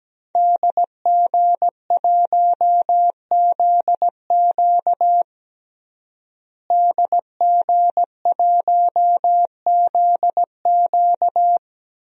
Morse code, heard as a keyed tone, about 17 wpm, 700 Hz. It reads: DG1ZQ DG1ZQ
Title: DXCC-100 Call Signs - 5 Characters - Repeated 2x - 17wpm